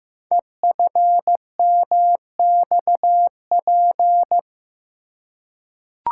Morse code, heard as a keyed tone, about 15 wpm, 700 Hz. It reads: EFMXP
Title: Code Groups - 5 Letters - 15wpm